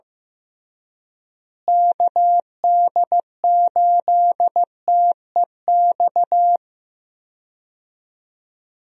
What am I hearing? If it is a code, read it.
Morse code, heard as a keyed tone, about 15 wpm, 700 Hz. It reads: KD8TEX
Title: USA Call Signs - 6 Characters - Repeated 3x - 15wpm